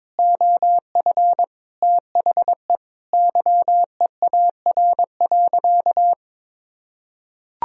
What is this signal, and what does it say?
Morse code, heard as a keyed tone, about 22 wpm, 700 Hz. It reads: OF THE YEAR.
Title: Sentences from Top 1000 Words - Review Part 1 - 22wpm